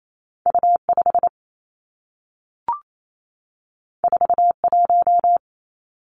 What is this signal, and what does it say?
Morse code, heard as a keyed tone, about 28 wpm, 700 Hz. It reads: U 5 4 1
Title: Two Challenging Letter-Number Combinations 28wpm